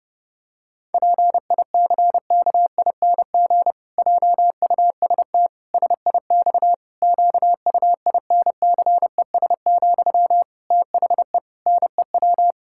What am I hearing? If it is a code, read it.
Morse code, heard as a keyed tone, about 30 wpm, 700 Hz. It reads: PICKING JUST SIX QUINCES, THE NEW
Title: Pangram Challenge v2 - 30wpm